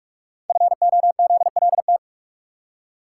Morse code, heard as a keyed tone, about 44 wpm, 700 Hz. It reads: FOZLT
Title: Code Groups - 5 Letters - 45wpm